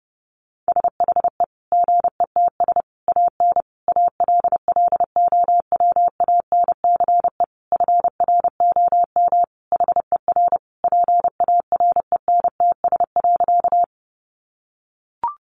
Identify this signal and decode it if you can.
Morse code, heard as a keyed tone, about 28 wpm, 700 Hz. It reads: SHE GETS AN ALLOWANCE FROM HER PARENTS.
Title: Easy Phrases 30wpm